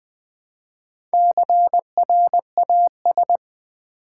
Morse code, heard as a keyed tone, about 20 wpm, 700 Hz. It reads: CRAS
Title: Build A Word - 1 Letter At A Time - 5-Letter Words - 20wpm